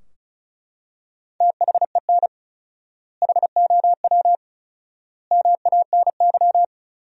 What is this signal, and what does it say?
Morse code, heard as a keyed tone, about 34 wpm, 700 Hz. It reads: THEN HOW MANY
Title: 4x Word-Spacing - Sentences from Top 100 Words - 35wpm